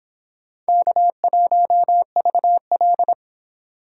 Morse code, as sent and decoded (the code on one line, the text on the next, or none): -.- .---- ...- .-..
K1VL